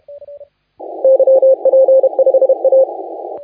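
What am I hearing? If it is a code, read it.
Morse code, heard as a keyed tone, about 31 wpm, 530 Hz. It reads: YP5A
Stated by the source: Morse contest audio recorded by LA4C during the SAC CW 2018 contest.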